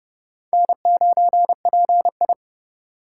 Morse code, heard as a keyed tone, about 29 wpm, 700 Hz. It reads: N9PI
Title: USA Call Signs - 2x Character Spacing - 30wpm